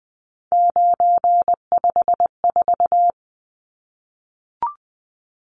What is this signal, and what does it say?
Morse code, heard as a keyed tone, about 20 wpm, 700 Hz. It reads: 954
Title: Numbers - 3-Digit 20wpm